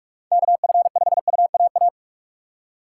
Morse code, heard as a keyed tone, about 59 wpm, spaced as for 52 wpm, 700 Hz. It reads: KF5UAA
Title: Speed Racing - USA 2x3 Call Signs 45wpm